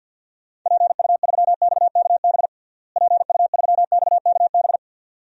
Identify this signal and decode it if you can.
Morse code, heard as a keyed tone, about 47 wpm, 700 Hz. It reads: PU3XKB PU3XKB
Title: DXCC-100 Call Signs - 6 Characters - Repeated 2x - 50wpm